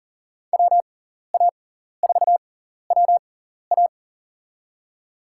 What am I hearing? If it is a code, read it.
Morse code, heard as a keyed tone, about 43 wpm, spaced as for 14 wpm, 690 Hz. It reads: WA4WA
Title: USA Call Signs - 5 Characters - 6x Character Spacing - 40wpm